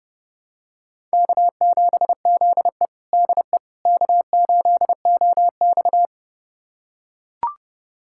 Morse code, heard as a keyed tone, about 30 wpm, 700 Hz. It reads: K7ZE DE K8OX
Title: USA Callsign DE Callsign - 4 Characters - 30wpm